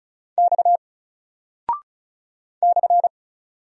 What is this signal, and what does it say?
Morse code, heard as a keyed tone, about 36 wpm, 700 Hz. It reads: X /
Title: Single Character (Letter:Number:Punctuation:Prosign) 35wpm